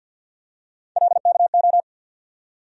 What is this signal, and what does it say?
Morse code, heard as a keyed tone, about 47 wpm, 700 Hz. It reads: LKO
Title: USA Call Sign Postfixes - 3 Letters 50wpm